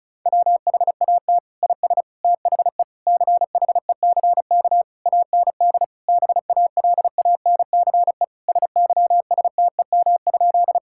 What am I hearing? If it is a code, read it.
Morse code, heard as a keyed tone, about 36 wpm, 700 Hz. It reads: WHAT IS THE CHECK AND BALANCE SYSTEM?
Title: Sentences from Top 1000 Words - Review Part 6 - 35wpm